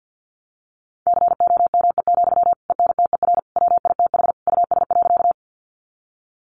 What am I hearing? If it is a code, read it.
Morse code, heard as a keyed tone, about 50 wpm, 700 Hz. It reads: COME, ENTER WITH US.
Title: Sentences from Top 1300 Words 50wpm